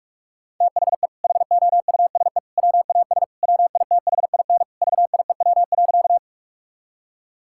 Morse code, heard as a keyed tone, about 46 wpm, 700 Hz. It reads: THE HOUSE WAS WITHIN VIEW.
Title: Sentences from Top 1000 Words - Review Part 6 - 45wpm